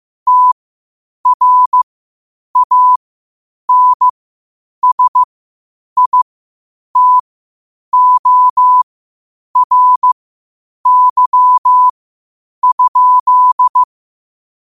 Morse code, heard as a keyed tone, about 15 wpm, spaced as for 8 wpm, 1000 Hz. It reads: TRANSITORY?
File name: qotdfiles / 8wpm